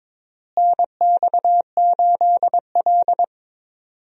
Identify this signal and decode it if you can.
Morse code, heard as a keyed tone, about 22 wpm, 700 Hz. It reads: NX8L
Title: USA Call Signs - 4 Characters - Repeated 2x - 22wpm